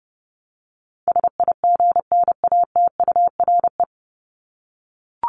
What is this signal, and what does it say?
Morse code, heard as a keyed tone, about 30 wpm, 700 Hz. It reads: SIGNATURE
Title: Top 1751-2000 Words 30wpm